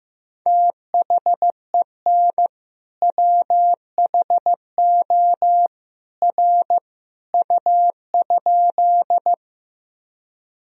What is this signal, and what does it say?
Morse code, heard as a keyed tone, about 15 wpm, 700 Hz. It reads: THEN WHO R U?
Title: Sentences from Top 100 Words and Abbreviations 15wpm